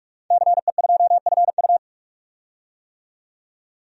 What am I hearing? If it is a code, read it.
Morse code, heard as a keyed tone, about 44 wpm, 690 Hz. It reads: CE2FU
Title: DXCC-100 Call Signs - 5 Characters - 45wpm